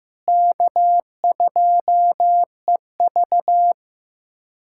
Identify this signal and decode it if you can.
Morse code, heard as a keyed tone, about 15 wpm, 700 Hz. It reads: K2EV
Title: USA Callsign DE Callsign - 4 Characters - 15wpm